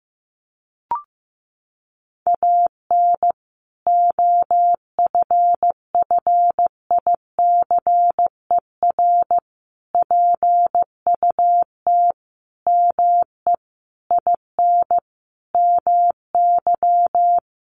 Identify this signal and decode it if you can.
Morse code, heard as a keyed tone, about 15 wpm, 700 Hz. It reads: AN OFFICER PUT ME IN MY
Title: Sentences from Top 1200 Words 15wpm